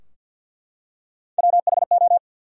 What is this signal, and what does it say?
Morse code, heard as a keyed tone, about 48 wpm, 700 Hz. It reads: WHO
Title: Top 100 Words 50wpm